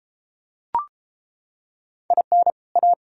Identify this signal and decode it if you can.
Morse code, heard as a keyed tone, about 33 wpm, 700 Hz. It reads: IN A